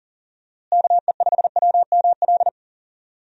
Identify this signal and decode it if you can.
Morse code, heard as a keyed tone, about 40 wpm, 700 Hz. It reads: KE5WML
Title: USA Call Signs - 6 Characters - Repeated 2x - 40wpm